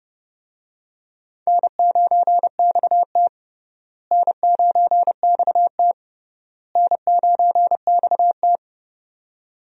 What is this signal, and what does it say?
Morse code, heard as a keyed tone, about 30 wpm, 700 Hz. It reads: N9XT N9XT N9XT
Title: USA Call Signs - 4 Characters - Repeated 3x - 30wpm